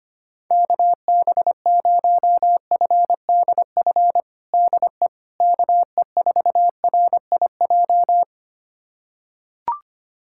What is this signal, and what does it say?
Morse code, heard as a keyed tone, about 25 wpm, 700 Hz. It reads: KB0FDF DE KE4RIJ
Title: USA Callsign DE Callsign 25wpm